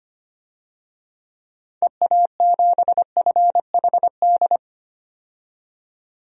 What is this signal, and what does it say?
Morse code, heard as a keyed tone, about 25 wpm, 700 Hz. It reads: EA7FHD
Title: DXCC-100 Call Signs - 6 Characters - 25wpm